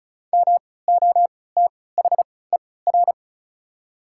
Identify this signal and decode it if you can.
Morse code, heard as a keyed tone, about 34 wpm, spaced as for 20 wpm, 690 Hz. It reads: MOTHER
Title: 3x Character Spacing - Top 200 Words - 35wpm